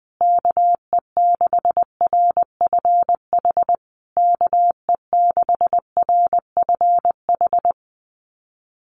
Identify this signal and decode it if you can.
Morse code, heard as a keyed tone, about 20 wpm, 700 Hz. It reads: KE6RFH KE6RFH
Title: USA Call Signs Repeated 3x 20wpm